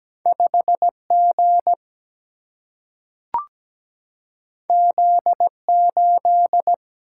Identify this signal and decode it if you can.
Morse code, heard as a keyed tone, about 17 wpm, 700 Hz. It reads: 5G Z8
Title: Single-Letter Single-Number 17wpm